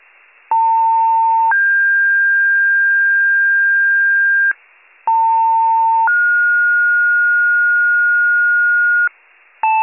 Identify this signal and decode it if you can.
Two-tone page, heard as a one-second tone, then a three-second tone, about 900 then 1600 Hz, about 900 then 1430 Hz, with noise. It is heard from 0.5 to 9.1 s.